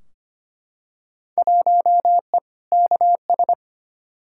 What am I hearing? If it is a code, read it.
Morse code, heard as a keyed tone, about 25 wpm, 700 Hz. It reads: 1E KS
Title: Field Day Exchanges - 25wpm